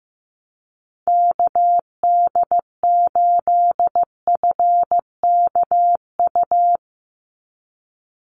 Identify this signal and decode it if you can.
Morse code, heard as a keyed tone, about 15 wpm, 700 Hz. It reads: KD8FKU
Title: USA Call Signs 15wpm